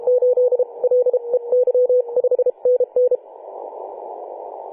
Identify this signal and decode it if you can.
Morse code, heard as a keyed tone, about 32 wpm, 500 Hz. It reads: 8 L E Y 5 NN
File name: 2013/Sep/21 2035z 3D2GC 18.072MHz CW